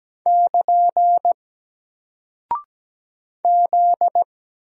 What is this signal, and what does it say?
Morse code, heard as a keyed tone, about 17 wpm, 700 Hz. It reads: <KN> Z
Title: Single Character (Letter:Number:Punctuation:Prosign) 17wpm